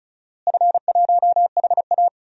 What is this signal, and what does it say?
Morse code, heard as a keyed tone, about 36 wpm, 700 Hz. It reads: F1HA
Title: DXCC-100 Call Signs - 4 Characters - Repeated 2x - 35wpm